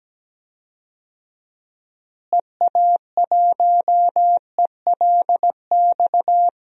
Morse code, heard as a keyed tone, about 17 wpm, 700 Hz. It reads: EA1ELX
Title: DXCC-100 Call Signs - 6 Characters - Repeated 2x - 17wpm